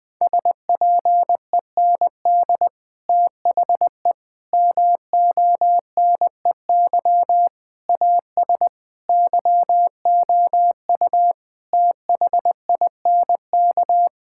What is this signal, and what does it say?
Morse code, heard as a keyed tone, about 21 wpm, 700 Hz. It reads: SPEND THE MONEY AS YOU THINK
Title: Sentences from Top 1000 Words - Review Part 5 - 20wpm